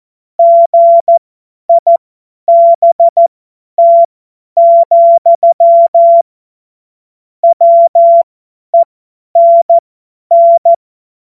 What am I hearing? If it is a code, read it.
Morse code, heard as a keyed tone, about 14 wpm, spaced as for 10 wpm, 680 Hz. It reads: GIBT, WENN
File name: Morsekurs / Witze / Witz 31. Januar 2023 50 F70